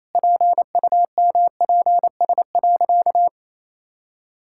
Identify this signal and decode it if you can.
Morse code, heard as a keyed tone, about 28 wpm, 700 Hz. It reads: PUMPS.
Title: Pangram Challenge v2 - 28wpm